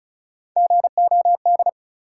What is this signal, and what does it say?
Morse code, heard as a keyed tone, about 36 wpm, 700 Hz. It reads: GOD
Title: Three Letter Words - 35wpm